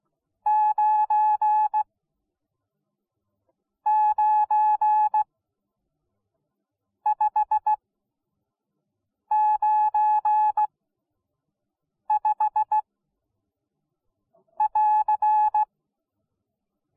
Morse code, heard as a keyed tone, about 16 wpm, spaced as for 5 wpm, 850 Hz. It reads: 99595 <AR>